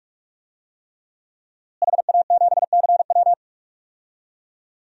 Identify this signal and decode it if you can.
Morse code, heard as a keyed tone, about 42 wpm, 700 Hz. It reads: HA7CW
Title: DXCC-100 Call Signs - 5 Characters - Repeated 3x - 45wpm